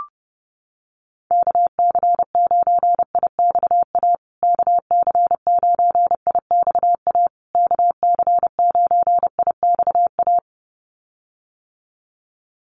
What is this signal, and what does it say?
Morse code, heard as a keyed tone, about 30 wpm, 700 Hz. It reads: KC9IXA KC9IXA KC9IXA
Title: USA Call Signs Repeated 3x 30wpm